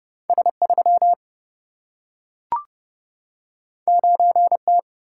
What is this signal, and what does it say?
Morse code, heard as a keyed tone, about 30 wpm, 700 Hz. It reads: S3 9T
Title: Single-Letter Single-Number 30wpm